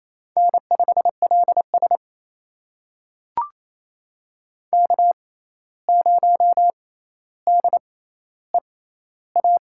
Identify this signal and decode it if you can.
Morse code, heard as a keyed tone, about 28 wpm, spaced as for 13 wpm, 700 Hz. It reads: N5LS K0DEA
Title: USA Call Signs - 6x Character Spacing - 28wpm